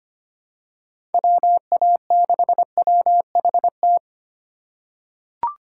Morse code, heard as a keyed tone, about 25 wpm, 700 Hz. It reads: WA6WHT
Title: USA Call Signs - 6 Characters - 3x Character Spacing - 25wpm